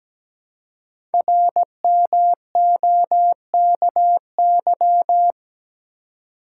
Morse code, heard as a keyed tone, about 17 wpm, 700 Hz. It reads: RMOKY
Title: Code Groups - 5 Letters - 17wpm